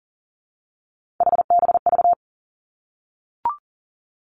Morse code, heard as a keyed tone, about 40 wpm, 700 Hz. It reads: HBV
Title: Three Challenging Letter-Number Combinations 40wpm